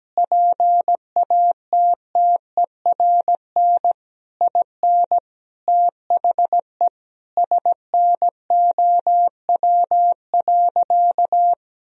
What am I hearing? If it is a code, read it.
Morse code, heard as a keyed tone, about 17 wpm, 700 Hz. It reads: PATTERN IN THE SNOW.
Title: Sentences from Top 1000 Words - Review Part 2 - 17wpm